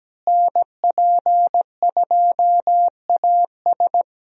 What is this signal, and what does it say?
Morse code, heard as a keyed tone, about 17 wpm, 700 Hz. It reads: NP2AS
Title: USA Call Signs - 5 Characters - Repeated 3x - 17wpm